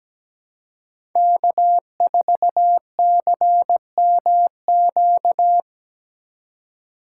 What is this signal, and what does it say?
Morse code, heard as a keyed tone, about 17 wpm, 700 Hz. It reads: K4CMQ
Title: USA Call Signs - 5 Characters - 4x Character Spacing - 17wpm